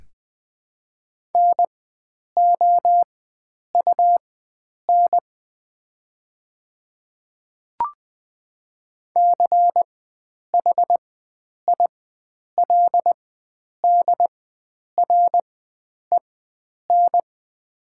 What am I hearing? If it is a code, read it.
Morse code, heard as a keyed tone, about 20 wpm, spaced as for 9 wpm, 700 Hz. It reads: NOUN CHILDREN
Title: 4x Character Spacing - Top 500 Words - 20wpm